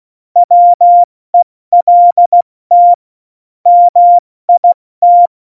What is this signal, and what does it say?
Morse code, heard as a keyed tone, about 16 wpm, 700 Hz. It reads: WELT MIT
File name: Morsekurs / Witze / Witz 24. November 2020, 70 F80